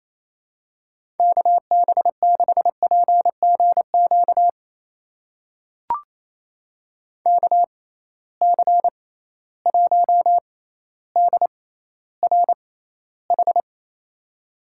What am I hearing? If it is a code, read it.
Morse code, heard as a keyed tone, about 28 wpm, spaced as for 14 wpm, 700 Hz. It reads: KB6PGQ KC1DRH
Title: USA Call Signs - 6x Character Spacing - 28wpm